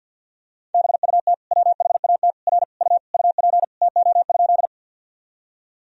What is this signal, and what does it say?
Morse code, heard as a keyed tone, about 51 wpm, 700 Hz. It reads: BUT WHAT R U UP TO?
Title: Sentences from Top 100 Words and Abbreviations 50wpm